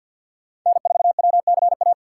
Morse code, heard as a keyed tone, about 50 wpm, 700 Hz. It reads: N4WCA
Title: USA Call Signs - 3x Character Spacing - 50wpm